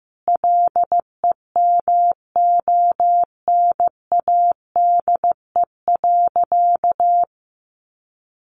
Morse code, heard as a keyed tone, about 15 wpm, 700 Hz. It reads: LEMONADE.
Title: Additional Easy Phrases 15wpm